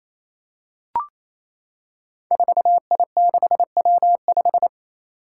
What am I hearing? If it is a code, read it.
Morse code, heard as a keyed tone, about 28 wpm, 700 Hz. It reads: 4I6W5
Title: Five Challenging Letter-Number Combinations 28wpm